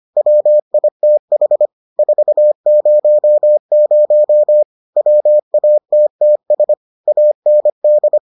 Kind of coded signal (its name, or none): Morse code